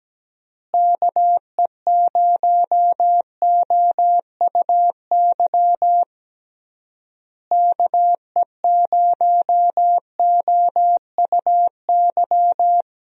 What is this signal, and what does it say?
Morse code, heard as a keyed tone, about 17 wpm, 700 Hz. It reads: KE0OUY KE0OUY
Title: USA Call Signs - 6 Characters - Repeated 3x - 17wpm